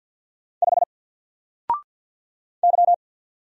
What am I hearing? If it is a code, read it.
Morse code, heard as a keyed tone, about 48 wpm, 700 Hz. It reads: 5 Y
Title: Single Letter-Number Alt 50wpm